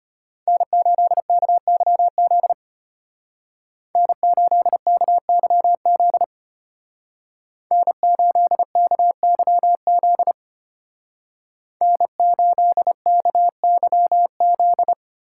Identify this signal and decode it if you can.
Morse code, heard as a keyed tone, about 30 wpm, 700 Hz. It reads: N8KYZ N8KYZ N8KYZ N8KYZ
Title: Speed Racing - USA 1x3 Call Signs 25wpm